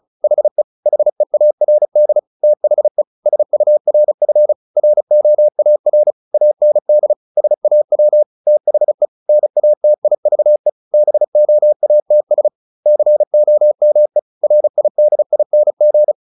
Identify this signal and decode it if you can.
Morse code, heard as a keyed tone, about 34 wpm, 600 Hz. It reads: HE HEARD THE SURF ROAR AND SAW THE NATIVE BOATS COME RIDING
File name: the old man and the sea / the old man and the sea 1 35wpm